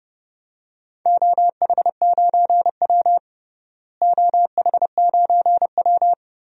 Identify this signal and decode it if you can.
Morse code, heard as a keyed tone, about 30 wpm, 700 Hz. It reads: OH9W OH9W
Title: DXCC-100 Call Signs - 4 Characters - Repeated 2x - 30wpm